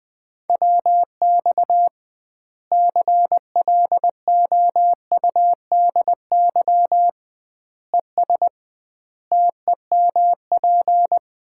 Morse code, heard as a keyed tone, about 21 wpm, 700 Hz. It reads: WX CLOUDY ES TEMP
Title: Common QSO Phrases - 2x Word-Spacing - 20wpm